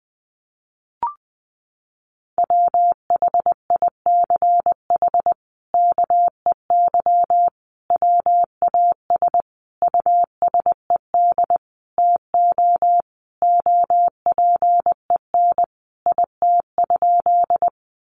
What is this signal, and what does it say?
Morse code, heard as a keyed tone, about 19 wpm, 700 Hz. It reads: WHICH KEY WAS USED TO OPEN IT?
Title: Sentences from Top 600 Words 20wpm